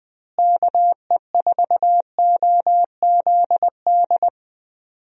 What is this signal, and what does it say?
Morse code, heard as a keyed tone, about 20 wpm, 700 Hz. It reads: KE4OZD
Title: USA Call Signs - 6 Characters - Repeated 3x - 20wpm